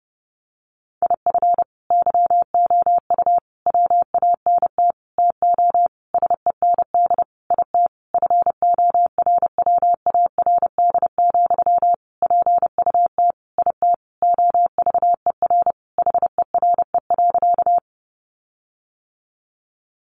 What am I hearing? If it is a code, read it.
Morse code, heard as a keyed tone, about 30 wpm, 700 Hz. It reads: IF YOU WANT TO SEND IT FORWARD, PUT IT OVER HERE.
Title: Sentences from Top 1500 Words 30wpm